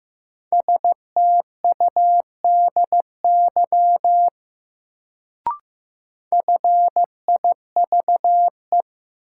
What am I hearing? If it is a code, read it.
Morse code, heard as a keyed tone, about 15 wpm, 700 Hz. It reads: STUDY FIVE
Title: Sets of 5 Words - New Word Drill - 15wpm